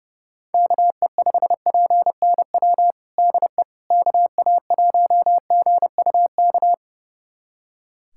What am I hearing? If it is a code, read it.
Morse code, heard as a keyed tone, about 30 wpm, 700 Hz. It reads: KE5PNW DE KA1GUK
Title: USA Callsign DE Callsign - 6 Characters - 30wpm